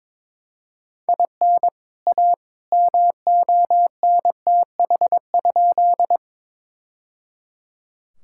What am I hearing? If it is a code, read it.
Morse code, heard as a keyed tone, about 22 wpm, 700 Hz. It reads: IN A MONTH?
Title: Sentences from Top 1000 Words - Review Part 5 - 22wpm